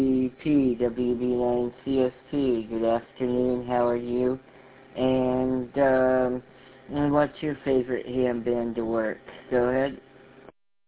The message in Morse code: none

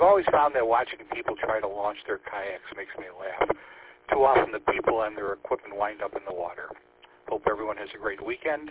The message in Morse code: none

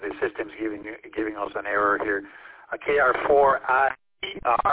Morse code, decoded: none

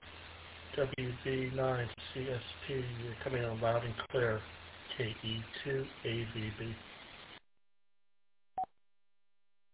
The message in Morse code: none

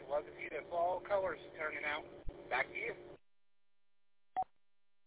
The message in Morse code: none

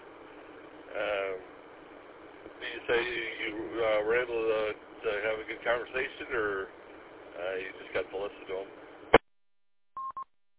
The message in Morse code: none

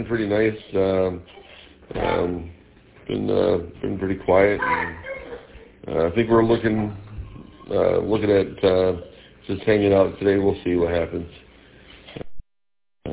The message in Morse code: none

